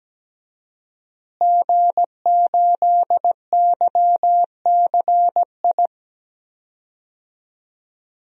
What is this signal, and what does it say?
Morse code, heard as a keyed tone, about 17 wpm, 700 Hz. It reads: G8YCI
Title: DXCC-100 Call Signs - 5 Characters - Repeated 2x - 17wpm